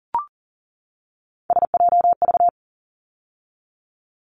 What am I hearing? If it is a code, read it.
Morse code, heard as a keyed tone, about 40 wpm, 700 Hz. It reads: SJV
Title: ARRL Sections 40wpm